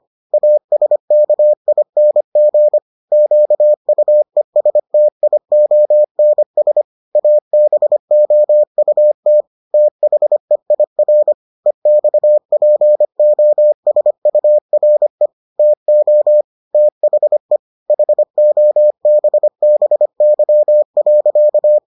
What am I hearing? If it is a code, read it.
Morse code, heard as a keyed tone, about 25 wpm, 600 Hz. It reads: ASKING QUESTIONS ABOUT THEIR EXPOSURE TO THE HOBBY.